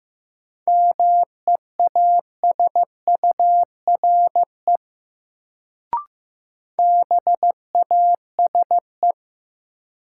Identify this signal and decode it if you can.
Morse code, heard as a keyed tone, about 15 wpm, 700 Hz. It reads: MEASURE BASE
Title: Sentences from Top 1000 Words - Review Part 1 - New Word Drill - 15wpm